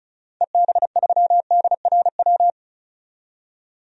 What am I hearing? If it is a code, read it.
Morse code, heard as a keyed tone, about 34 wpm, 700 Hz. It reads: EB3DRW
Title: DXCC-100 Call Signs - 6 Characters - 35wpm